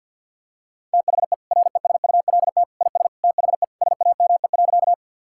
Morse code, heard as a keyed tone, about 50 wpm, 700 Hz. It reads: THE RESULT IS THE SAME.
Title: Sentences from Top 1000 Words - Review Part 5 - 50wpm